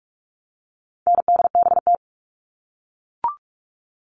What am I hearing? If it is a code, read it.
Morse code, heard as a keyed tone, about 44 wpm, 700 Hz. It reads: NDBT
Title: All Cut Numbers - 4-Digit 45wpm